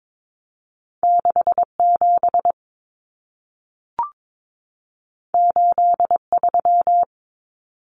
Morse code, heard as a keyed tone, about 22 wpm, 700 Hz. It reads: 67 83
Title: Numbers - 2-Digit 22wpm